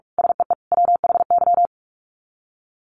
Morse code, heard as a keyed tone, about 44 wpm, 700 Hz. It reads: SEE WHY
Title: Sets of 5 Words Encore 45wpm